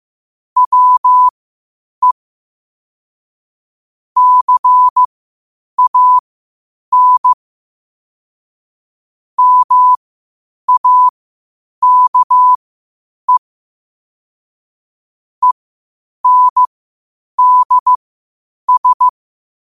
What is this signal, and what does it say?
Morse code, heard as a keyed tone, about 15 wpm, spaced as for 8 wpm, 1000 Hz. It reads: WE CAN MAKE ENDS
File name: qotdfiles / 8wpm